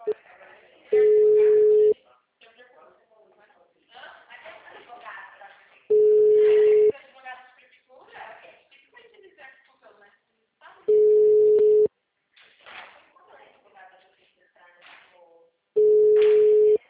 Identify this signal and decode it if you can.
Ringback tone (Europe, 425 Hz), with speech.
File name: out- -2008- - - .63